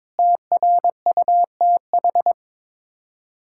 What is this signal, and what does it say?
Morse code, heard as a keyed tone, about 22 wpm, 700 Hz. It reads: TRUTH
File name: practice / 22wpm / Sets of 8 Words - 22wpm